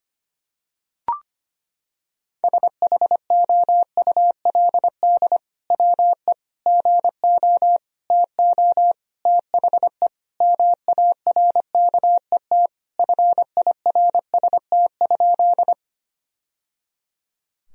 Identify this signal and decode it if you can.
Morse code, heard as a keyed tone, about 25 wpm, 700 Hz. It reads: SHOULD WE GO TO THE MARKET FIRST?
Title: Sentences from Top 1000 Words - Review Part 3 - 25wpm